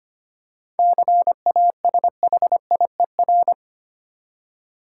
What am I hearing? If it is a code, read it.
Morse code, heard as a keyed tone, about 25 wpm, 700 Hz. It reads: CASHIER
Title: Practice QSO - Exchange 3 - More Info - 25wpm